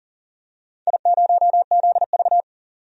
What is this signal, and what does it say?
Morse code, heard as a keyed tone, about 40 wpm, 700 Hz. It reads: I0ZV
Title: DXCC-100 Call Signs - 4 Characters - 40wpm